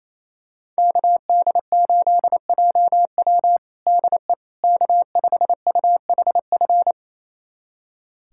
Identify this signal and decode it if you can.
Morse code, heard as a keyed tone, about 28 wpm, 700 Hz. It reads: KD8JW DE K5UHF
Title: USA Callsign DE Callsign - 5 Characters - 28wpm